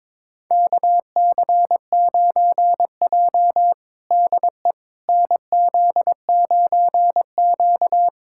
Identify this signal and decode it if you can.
Morse code, heard as a keyed tone, about 22 wpm, 700 Hz. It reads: KC9J DE NZ9Q
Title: USA Callsign DE Callsign - 4 Characters - 22wpm